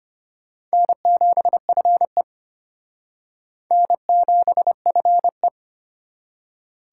Morse code, heard as a keyed tone, about 32 wpm, 700 Hz. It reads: N7FE N7FE